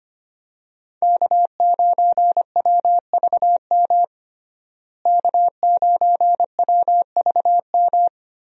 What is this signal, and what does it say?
Morse code, heard as a keyed tone, about 25 wpm, 700 Hz. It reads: K9WVM K9WVM
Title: USA Call Signs - 5 Characters - Repeated 3x - 25wpm